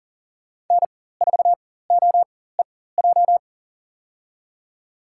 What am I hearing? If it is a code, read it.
Morse code, heard as a keyed tone, about 37 wpm, spaced as for 18 wpm, 700 Hz. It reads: N4OEJ
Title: USA Call Signs - 5 Characters - 4x Character Spacing - 40wpm